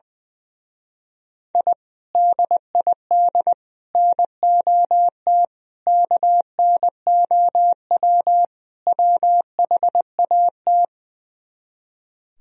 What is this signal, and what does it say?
Morse code, heard as a keyed tone, about 20 wpm, 700 Hz. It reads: I DID NOT KNOW WHAT
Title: Speed-Racing - Sets of 5 Words 20wpm